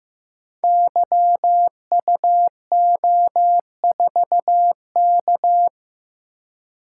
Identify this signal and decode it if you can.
Morse code, heard as a keyed tone, about 15 wpm, 700 Hz. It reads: YUO4K
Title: Code Groups - 5 Letter:Numbers - 15wpm